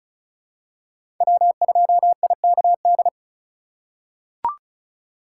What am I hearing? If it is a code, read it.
Morse code, heard as a keyed tone, about 36 wpm, 700 Hz. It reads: W2IKD
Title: USA Call Signs - 4x Character Spacing - 35wpm